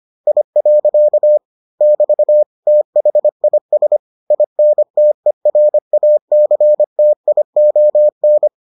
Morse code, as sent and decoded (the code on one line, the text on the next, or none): .. .-.-.- / -...- / - .... .. ... / .. -. - . .-. .- -.-. - .. --- -.
I. <BT> THIS INTERACTION